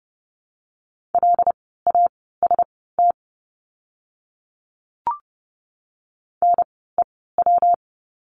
Morse code, heard as a keyed tone, about 30 wpm, spaced as for 17 wpm, 700 Hz. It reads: LAST NEW
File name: course / 30:10wpm / 115 Lesson 19f - Letter w - Words 30wpm